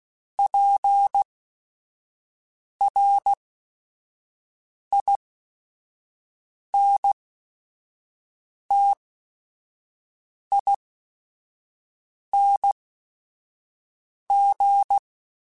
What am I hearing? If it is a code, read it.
Morse code, heard as a keyed tone, about 16 wpm, spaced as for 5 wpm, 800 Hz. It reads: PRINTING